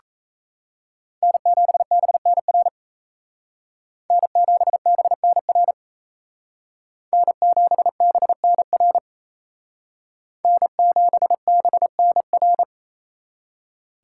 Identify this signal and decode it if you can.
Morse code, heard as a keyed tone, about 36 wpm, 700 Hz. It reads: N7BNR N7BNR N7BNR N7BNR